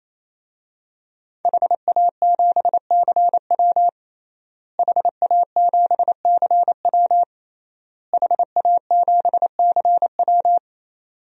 Morse code, heard as a keyed tone, about 28 wpm, 700 Hz. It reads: HA7CW HA7CW HA7CW
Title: DXCC-100 Call Signs - 5 Characters - Repeated 3x - 28wpm